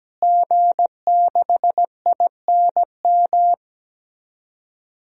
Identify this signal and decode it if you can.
Morse code, heard as a keyed tone, about 17 wpm, 700 Hz. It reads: G6INM
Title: DXCC-100 Call Signs - 5 Characters - Repeated 2x - 17wpm